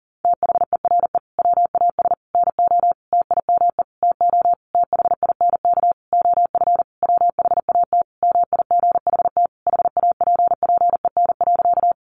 Morse code, heard as a keyed tone, about 40 wpm, 700 Hz. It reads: THERE WAS NO TIME TO THINK OF WHAT MIGHT HAPPEN.
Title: Sentences from Top 400 Words 40wpm